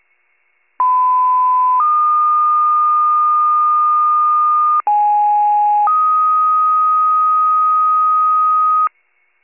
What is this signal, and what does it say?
Two-tone page, heard as a one-second tone, then a three-second tone, about 1010 then 1220 Hz, about 830 then 1250 Hz. It is heard from 0.8 to 8.9 s.